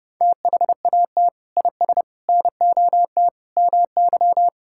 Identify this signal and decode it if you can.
Morse code, heard as a keyed tone, about 30 wpm, 700 Hz. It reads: THAT IS NOT MY
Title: Sentences from Top 1000 Words - Review Part 6 - 30wpm